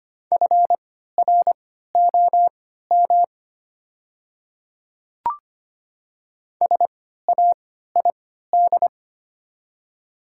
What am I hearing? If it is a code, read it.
Morse code, heard as a keyed tone, about 25 wpm, spaced as for 14 wpm, 700 Hz. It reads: FROM SAID
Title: 3x Character Spacing - Top 200 Words - 25wpm